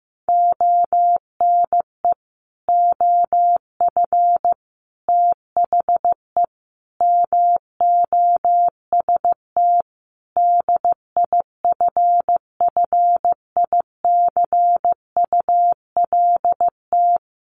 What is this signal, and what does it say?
Morse code, heard as a keyed tone, about 15 wpm, 700 Hz. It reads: ONE OF THE MOST DIFFICULT
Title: Sets of 5 Words Encore2 15wpm